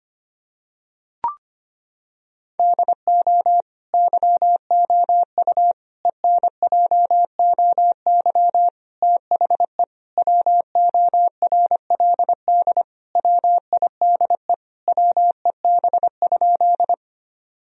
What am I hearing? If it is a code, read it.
Morse code, heard as a keyed tone, about 25 wpm, 700 Hz. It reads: DO YOU ENJOY THE WORLD WIDE WEB?
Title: Sentences from Top 1000 Words - Review Part 4 - 25wpm